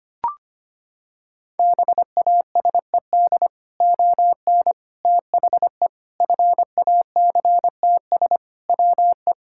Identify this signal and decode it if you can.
Morse code, heard as a keyed tone, about 25 wpm, 700 Hz. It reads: BASED ON THE FACTS WE
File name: practice / 25wpm / Sentences with 3-Word Phrases - 25wpm